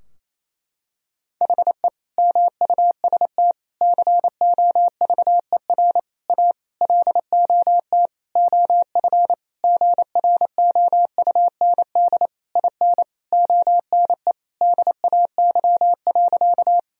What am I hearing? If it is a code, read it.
Morse code, heard as a keyed tone, about 27 wpm, 700 Hz. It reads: HE MUST COVER A LOT OF GROUND IN ONE DAY.
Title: Sentences from Top 1000 Words - Review Part 3 - 28wpm